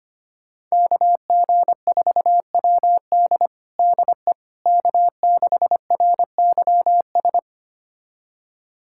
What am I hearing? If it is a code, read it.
Morse code, heard as a keyed tone, about 25 wpm, 700 Hz. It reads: KG4WD DE K6RYS
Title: USA Callsign DE Callsign - 5 Characters - 25wpm